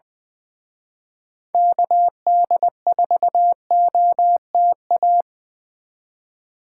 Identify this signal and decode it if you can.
Morse code, heard as a keyed tone, about 25 wpm, 700 Hz. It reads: KD4OTA
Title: Speed Racing - USA 2x3 Call Signs 20wpm